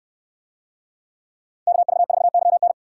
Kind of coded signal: Morse code

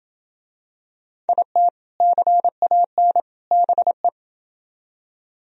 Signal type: Morse code

